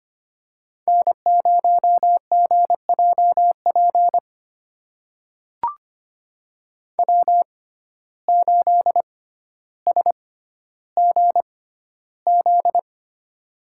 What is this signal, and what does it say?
Morse code, heard as a keyed tone, about 25 wpm, spaced as for 13 wpm, 700 Hz. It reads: N0GJP W8SGZ
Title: USA Call Signs - 5 Characters - 6x Character Spacing - 25wpm